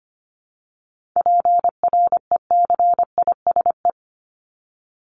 Morse code, heard as a keyed tone, about 25 wpm, 700 Hz. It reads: PRECISE
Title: LY Suffix - Top 100 Words with Base Words 25wpm